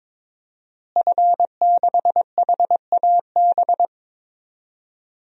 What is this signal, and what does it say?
Morse code, heard as a keyed tone, about 22 wpm, 700 Hz. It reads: F6HAB
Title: DXCC-100 Call Signs - 5 Characters - Repeated 3x - 22wpm